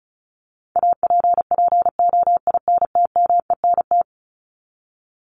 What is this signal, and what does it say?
Morse code, heard as a keyed tone, about 34 wpm, 700 Hz. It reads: APPOINTMENT
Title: Top 1751-2000 Words 35wpm